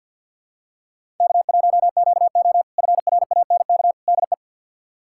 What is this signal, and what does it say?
Morse code, heard as a keyed tone, about 53 wpm, 700 Hz. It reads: K1QO FRANK DE
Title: K1USN SST QSO Practice - 50wpm